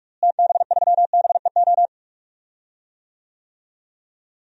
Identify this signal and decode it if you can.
Morse code, heard as a keyed tone, about 44 wpm, 700 Hz. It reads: TB3BEO
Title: DXCC-100 Call Signs - 6 Characters - Repeated 2x - 45wpm